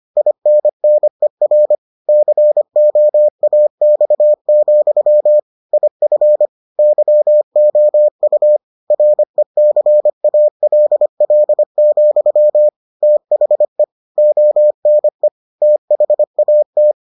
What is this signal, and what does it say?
Morse code, heard as a keyed tone, about 25 wpm, 600 Hz. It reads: INNER COAX, IF YOU RECALL, THE ONE THAT